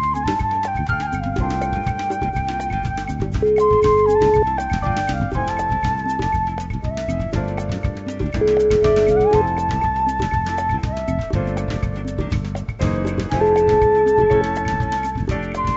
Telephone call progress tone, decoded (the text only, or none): ringback tone